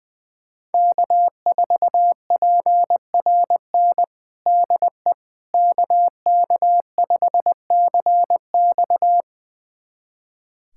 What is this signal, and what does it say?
Morse code, heard as a keyed tone, about 20 wpm, 700 Hz. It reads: K4PRN DE KK5CX
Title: USA Callsign DE Callsign - 5 Characters - 20wpm